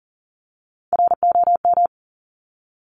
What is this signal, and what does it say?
Morse code, heard as a keyed tone, about 41 wpm, 700 Hz. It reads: ROM
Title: Top Three Letter Combinations in Words 40wpm